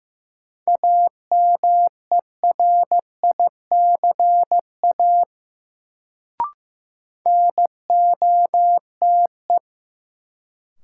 Morse code, heard as a keyed tone, about 15 wpm, 700 Hz. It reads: AMERICA NOTE
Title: Sets of 5 Words - New Word Drill - 15wpm